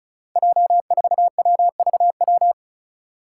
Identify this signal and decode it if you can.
Morse code, heard as a keyed tone, about 34 wpm, 700 Hz. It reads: J4WVW
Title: Five Challenging Letter-Number Combinations 35wpm